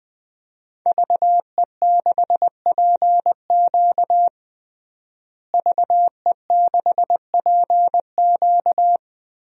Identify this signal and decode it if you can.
Morse code, heard as a keyed tone, about 20 wpm, 700 Hz. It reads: VE6PQ VE6PQ
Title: DXCC-100 Call Signs - 5 Characters - Repeated 3x - 20wpm